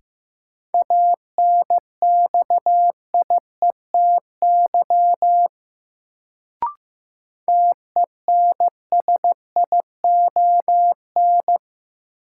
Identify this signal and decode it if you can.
Morse code, heard as a keyed tone, about 15 wpm, 700 Hz. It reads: ANXIETY TENSION
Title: Sentences from Top 1600 Words - New Word Drill - 15wpm